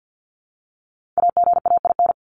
Morse code, heard as a keyed tone, about 48 wpm, 700 Hz. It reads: AGAIN
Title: Top 200 Words 50wpm